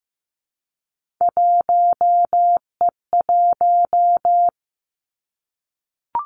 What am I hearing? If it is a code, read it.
Morse code, heard as a keyed tone, about 15 wpm, 700 Hz. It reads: 1E1
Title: Three Challenging Letter-Number Combinations 15wpm